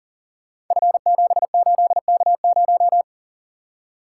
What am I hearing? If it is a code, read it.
Morse code, heard as a keyed tone, about 38 wpm, 690 Hz. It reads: F78K0
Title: Code Groups - 5 Letter:Numbers - 40wpm